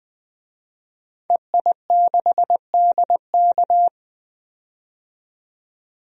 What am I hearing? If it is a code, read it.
Morse code, heard as a keyed tone, about 20 wpm, 700 Hz. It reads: EI6DK
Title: DXCC-100 Call Signs - 5 Characters - 20wpm